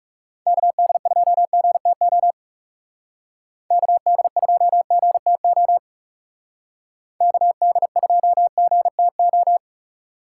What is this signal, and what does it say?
Morse code, heard as a keyed tone, about 39 wpm, 700 Hz. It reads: KD2GTO KD2GTO KD2GTO